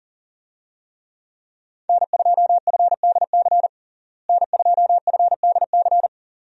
Morse code, heard as a keyed tone, about 40 wpm, 700 Hz. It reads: N2FDC N2FDC
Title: USA Call Signs - 5 Characters - Repeated 2x - 40wpm